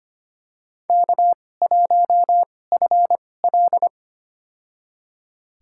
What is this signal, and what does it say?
Morse code, heard as a keyed tone, about 25 wpm, spaced as for 18 wpm, 700 Hz. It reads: K1FL
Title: USA Call Signs - 2x Character Spacing - 25wpm